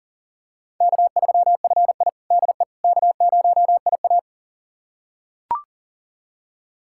Morse code, heard as a keyed tone, about 40 wpm, 700 Hz. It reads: K3FI DE K0IA
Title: USA Callsign DE Callsign - 4 Characters - 40wpm